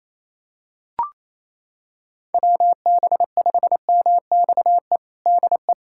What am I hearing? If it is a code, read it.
Morse code, heard as a keyed tone, about 28 wpm, 700 Hz. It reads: WB5MXE DE
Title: USA Callsign DE Callsign - 6 Characters - 28wpm